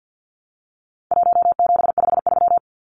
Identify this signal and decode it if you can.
Morse code, heard as a keyed tone, about 48 wpm, 700 Hz. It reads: 1753
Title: Numbers - 4-Digit 50wpm